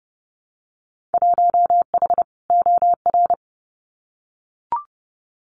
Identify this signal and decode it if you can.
Morse code, heard as a keyed tone, about 30 wpm, 700 Hz. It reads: 1H OR
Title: Winter Field Day Exchanges 30wpm